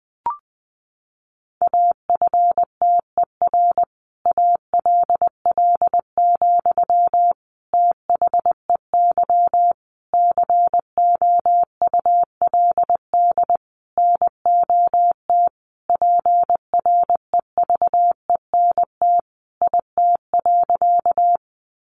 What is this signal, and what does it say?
Morse code, heard as a keyed tone, about 20 wpm, 700 Hz. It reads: AFTER ALL, THEY COULD NOT PREVENT IT.
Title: Sentences from Top 1100 Words 20wpm